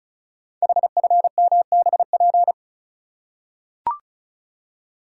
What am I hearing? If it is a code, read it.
Morse code, heard as a keyed tone, about 36 wpm, 700 Hz. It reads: HFMBP
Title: Code Groups - 5 Letter:Numbers - 35wpm